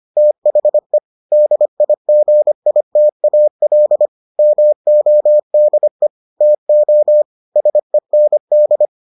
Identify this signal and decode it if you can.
Morse code, heard as a keyed tone, about 25 wpm, 600 Hz. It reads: THE DIGITAL MODE TO SEND